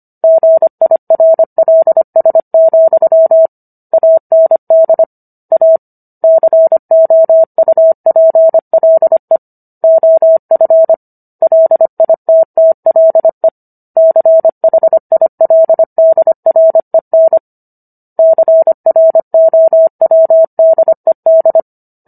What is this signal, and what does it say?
Morse code, heard as a keyed tone, about 25 wpm, 650 Hz. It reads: GIRLS, AND A COUPLE OF LITTLE CHILDREN CROWDED
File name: Book One / 16-Chapter 16 - 25 WPM